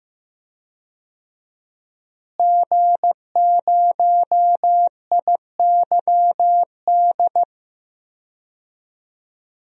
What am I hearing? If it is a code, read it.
Morse code, heard as a keyed tone, about 16 wpm, 700 Hz. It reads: G0IYD
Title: DXCC-100 Call Signs - 5 Characters - Repeated 3x - 15wpm